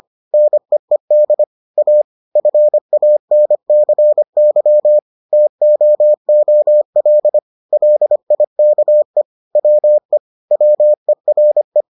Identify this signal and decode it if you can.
Morse code, heard as a keyed tone, about 25 wpm, 600 Hz. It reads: NEED A FANCY TOOL LIKE WE WERE